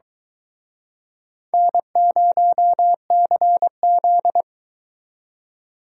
Morse code, heard as a keyed tone, about 24 wpm, 700 Hz. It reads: N0CZ